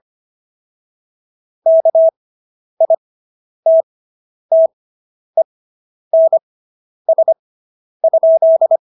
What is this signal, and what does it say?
Morse code, heard as a keyed tone, about 25 wpm, spaced as for 10 wpm, 650 Hz. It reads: KITTENS?